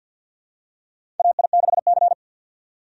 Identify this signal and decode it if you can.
Morse code, heard as a keyed tone, about 47 wpm, 700 Hz. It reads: AI6C
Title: USA Call Signs - 4 Characters - 50wpm